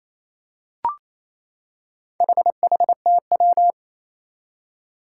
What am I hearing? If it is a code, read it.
Morse code, heard as a keyed tone, about 27 wpm, 700 Hz. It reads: HHTW
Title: Four Challenging Letter-Number Combinations 28wpm